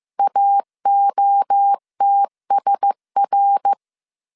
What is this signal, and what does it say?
Morse code, heard as a keyed tone, about 15 wpm, 800 Hz. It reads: AOTSR
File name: CW / DET / 1 6 1500C S A 14wpm 2921